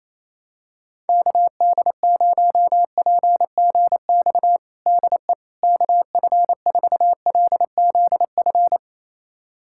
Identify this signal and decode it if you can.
Morse code, heard as a keyed tone, about 28 wpm, 700 Hz. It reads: KD0PGX DE KF4LZF
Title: USA Callsign DE Callsign 28wpm